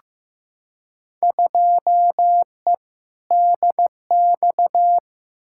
Morse code, heard as a keyed tone, about 15 wpm, 700 Hz. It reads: 2E DX
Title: Field Day Exchanges - 15wpm